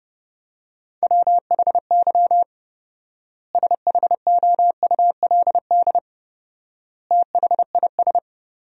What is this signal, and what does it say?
Morse code, heard as a keyed tone, about 32 wpm, 700 Hz. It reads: WHY SHOULD THIS
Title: 4x Word-Spacing - Sentences from Top 300 Words - 30wpm